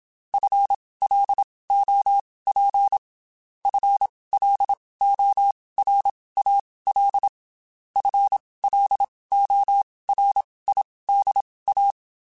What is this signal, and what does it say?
Morse code, heard as a keyed tone, about 27 wpm, spaced as for 19 wpm, 800 Hz. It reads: FLOP FLORAL FLORIDA
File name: Spaced / FL Runs sp mp3s / FL Runs sp @25